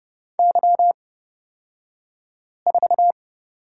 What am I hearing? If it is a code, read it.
Morse code, heard as a keyed tone, about 31 wpm, spaced as for 5 wpm, 700 Hz. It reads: Y4
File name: Lesson 37d - Letter x - Two Characters 30wpm